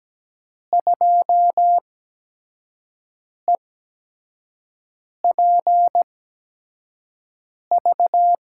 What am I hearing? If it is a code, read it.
Morse code, heard as a keyed tone, about 17 wpm, spaced as for 5 wpm, 700 Hz. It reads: 2EPV
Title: USA Call Signs - 8x Character Spacing - 17wpm